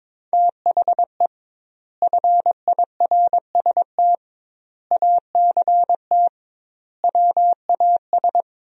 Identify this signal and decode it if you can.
Morse code, heard as a keyed tone, about 22 wpm, 700 Hz. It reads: THE FIRST ACT WAS
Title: 2x Word-Spacing - Sentences from Top 200 Words - 22wpm